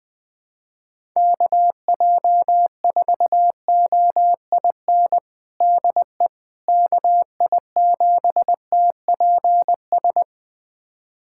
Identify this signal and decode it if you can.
Morse code, heard as a keyed tone, about 20 wpm, 700 Hz. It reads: KJ4OIN DE KI7TPS
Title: USA Callsign DE Callsign 20wpm